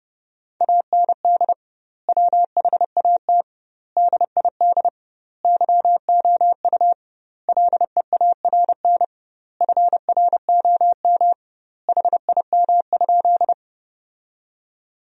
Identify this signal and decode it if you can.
Morse code, heard as a keyed tone, about 30 wpm, 700 Hz. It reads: AND WHAT DID YOU LEARN FROM HIM?
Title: 2x Word-Spacing - Sentences from Top 300 Words - 30wpm